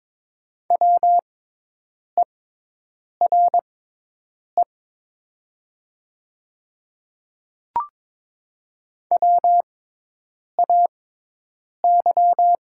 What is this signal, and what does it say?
Morse code, heard as a keyed tone, about 22 wpm, spaced as for 8 wpm, 700 Hz. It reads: WERE WAY
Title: 6x Character Spacing - Top 100 Words - 22wpm